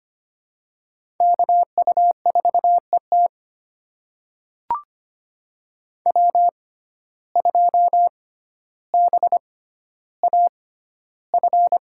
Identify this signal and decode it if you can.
Morse code, heard as a keyed tone, about 25 wpm, spaced as for 13 wpm, 700 Hz. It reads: KU4ET W2BAF
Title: USA Call Signs - 5 Characters - 6x Character Spacing - 25wpm